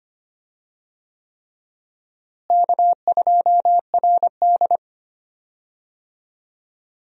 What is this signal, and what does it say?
Morse code, heard as a keyed tone, about 25 wpm, 700 Hz. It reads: K2RD
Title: USA Call Signs - 4 Characters - Repeated 3x - 25wpm